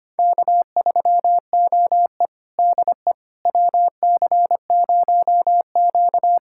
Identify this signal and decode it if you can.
Morse code, heard as a keyed tone, about 25 wpm, 700 Hz. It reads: K3OE DE WC0Q
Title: USA Callsign DE Callsign - 4 Characters - 25wpm